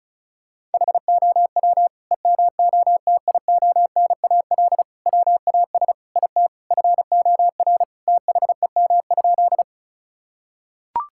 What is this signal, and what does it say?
Morse code, heard as a keyed tone, about 34 wpm, 700 Hz. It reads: HOW EMOTIONAL WAS IT FOR THEM?
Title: Sentences from Top 1000 Words - Review Part 5 - 35wpm